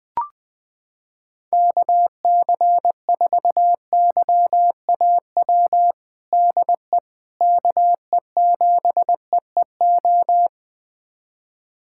Morse code, heard as a keyed tone, about 21 wpm, 700 Hz. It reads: KC4YAW DE KE7EEO
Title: USA Callsign DE Callsign 20wpm